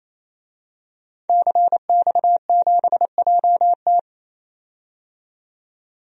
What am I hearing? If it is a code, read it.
Morse code, heard as a keyed tone, about 28 wpm, 700 Hz. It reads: CX7JT
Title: DXCC-100 Call Signs - 5 Characters - 28wpm